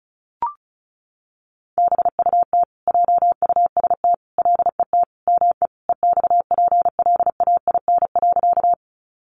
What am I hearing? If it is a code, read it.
Morse code, heard as a keyed tone, about 34 wpm, 700 Hz. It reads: BUT JUST LET ME EXPLAIN.
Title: Sentences from Top 1100 Words 35wpm